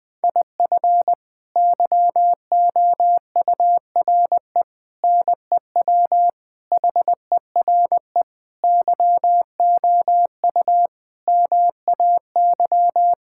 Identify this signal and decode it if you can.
Morse code, heard as a keyed tone, about 20 wpm, 700 Hz. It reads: IF YOURE NEW HERE YOU MAY
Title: Sets of 8 Words - 20wpm